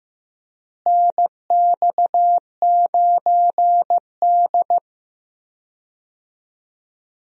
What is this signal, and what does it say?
Morse code, heard as a keyed tone, about 15 wpm, 700 Hz. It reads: NX9D
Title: USA Call Signs - 4 Characters - Repeated 2x - 15wpm